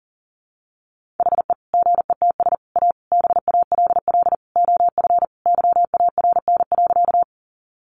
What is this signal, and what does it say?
Morse code, heard as a keyed tone, about 40 wpm, 700 Hz. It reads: HE GETS A BALL OF YARN.
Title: Additional Easy Phrases 40wpm